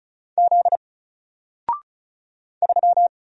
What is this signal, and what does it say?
Morse code, heard as a keyed tone, about 34 wpm, 700 Hz. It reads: Z 3
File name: Single-Letter-Number-35wpm